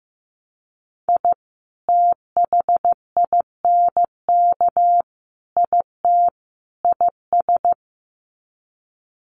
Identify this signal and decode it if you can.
Morse code, heard as a keyed tone, about 15 wpm, 700 Hz. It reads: I THINK IT IS
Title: Sets of 4 Words 15wpm